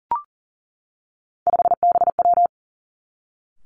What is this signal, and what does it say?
Morse code, heard as a keyed tone, about 40 wpm, 700 Hz. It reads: 5BW
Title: Three Challenging Letter-Number Combinations 40wpm